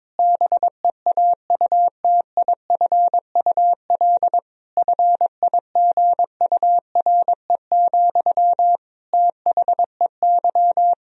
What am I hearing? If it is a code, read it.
Morse code, heard as a keyed tone, about 22 wpm, 700 Hz. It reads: BEAUTIFUL FIGURE, THEY
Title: Sentences from Top 1000 Words - Review Part 2 - 22wpm